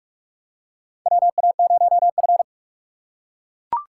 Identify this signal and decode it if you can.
Morse code, heard as a keyed tone, about 42 wpm, 700 Hz. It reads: WA0F
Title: USA Call Signs - 6x Character Spacing - 45wpm